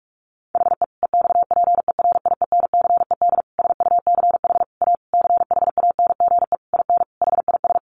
Morse code, heard as a keyed tone, about 46 wpm, 700 Hz. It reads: HE EXPERIENCED SUCH A CHANGE IN HIS
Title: Sentences from Top 1100 Words 45wpm